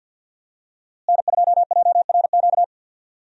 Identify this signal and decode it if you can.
Morse code, heard as a keyed tone, about 50 wpm, 700 Hz. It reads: N2JRQ
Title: USA Call Signs - 5 Characters - Repeated 3x - 50wpm